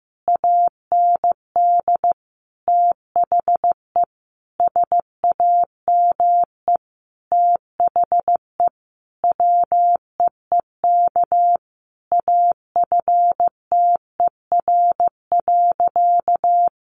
Morse code, heard as a keyed tone, about 15 wpm, 700 Hz. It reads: AND THE SAME THE WEEK AFTER.
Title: Sentences from Top 500 Words 15wpm